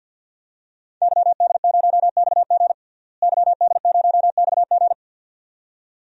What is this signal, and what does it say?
Morse code, heard as a keyed tone, about 47 wpm, 700 Hz. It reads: YD0XG YD0XG
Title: DXCC-100 Call Signs - 5 Characters - Repeated 2x - 50wpm